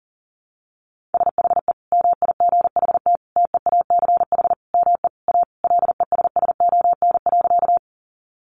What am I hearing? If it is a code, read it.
Morse code, heard as a keyed tone, about 37 wpm, 700 Hz. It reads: SHE MIGHT TEACH ME A LESSON.
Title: Sentences from Top 1200 Words 40wpm